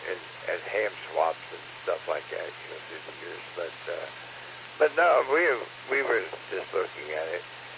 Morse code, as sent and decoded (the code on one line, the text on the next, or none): none